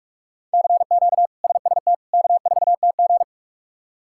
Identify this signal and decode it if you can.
Morse code, heard as a keyed tone, about 46 wpm, 700 Hz. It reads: CQ SST K4TG
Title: K1USN SST QSO Practice - 45wpm